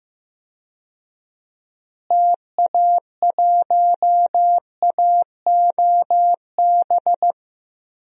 Morse code, heard as a keyed tone, about 15 wpm, 700 Hz. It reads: TA1AOB
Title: DXCC-100 Call Signs - 6 Characters - Repeated 2x - 15wpm